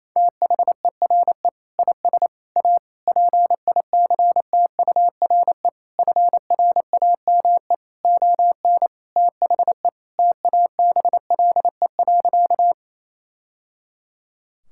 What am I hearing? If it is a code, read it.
Morse code, heard as a keyed tone, about 28 wpm, 700 Hz. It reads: THERE IS A PICTURE FRAME ON THE TABLE.
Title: Sentences from Top 1000 Words - Review Part 6 - 28wpm